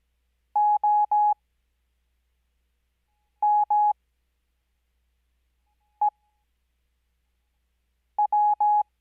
Morse code, heard as a keyed tone, about 17 wpm, spaced as for 4 wpm, 840 Hz. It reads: OMEW
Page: Farnsworth Spacing: 15WPM Character speed Spaced at 5WPM.